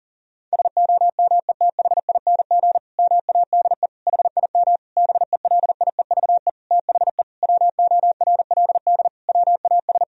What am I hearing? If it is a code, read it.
Morse code, heard as a keyed tone, about 40 wpm, 700 Hz. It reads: SOMETHING MADE HIM BELIEVE THE WORLD WAS
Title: Sentences from Top 1000 Words - Review Part 1 - 40wpm